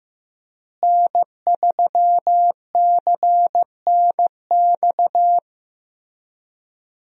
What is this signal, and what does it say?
Morse code, heard as a keyed tone, about 15 wpm, 700 Hz. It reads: N3CNX
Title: USA Call Signs - 5 Characters - Repeated 2x - 15wpm